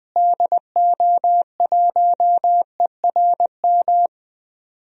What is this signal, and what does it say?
Morse code, heard as a keyed tone, about 21 wpm, 700 Hz. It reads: DO1ERM
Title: DXCC-100 Call Signs - 6 Characters - Repeated 3x - 20wpm